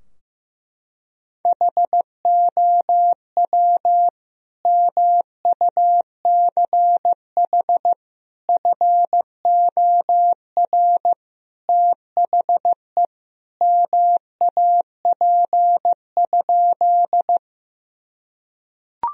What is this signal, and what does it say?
Morse code, heard as a keyed tone, about 16 wpm, 700 Hz. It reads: HOW MUCH FOR THE MAP?
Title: Sentences from Top 1000 Words - Review Part 2 - 15wpm